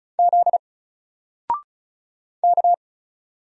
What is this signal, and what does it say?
Morse code, heard as a keyed tone, about 36 wpm, 700 Hz. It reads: Z K
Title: Single Character (Letter:Number:Punctuation:Prosign) Alt 35wpm